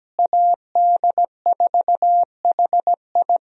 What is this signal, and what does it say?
Morse code, heard as a keyed tone, about 17 wpm, 700 Hz. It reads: AD4HI
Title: USA Call Signs - 5 Characters - Repeated 3x - 17wpm